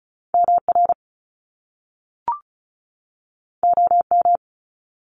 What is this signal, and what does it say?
Morse code, heard as a keyed tone, about 36 wpm, 700 Hz. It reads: MR OM
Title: Top Two Letter Combinations in Words 35wpm